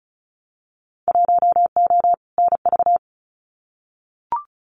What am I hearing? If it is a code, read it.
Morse code, heard as a keyed tone, about 36 wpm, 700 Hz. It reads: 1O NV
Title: Winter Field Day Exchanges Encore 35wpm